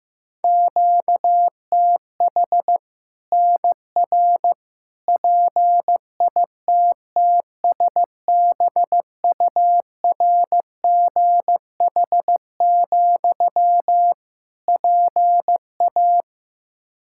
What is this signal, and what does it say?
Morse code, heard as a keyed tone, about 15 wpm, 700 Hz. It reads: QTH NR PITTSBURGH, PA
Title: Practice QSO - Exchange 1 - Beginning - 15wpm